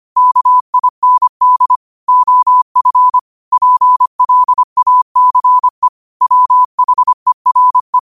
Morse code, heard as a keyed tone, about 25 wpm, 1000 Hz. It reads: KIND OF PLACE WHERE